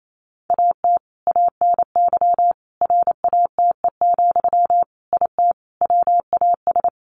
Morse code, heard as a keyed tone, about 28 wpm, 700 Hz. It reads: AT ANY RATE, IT WAS
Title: Sentences from Top 1200 Words 28wpm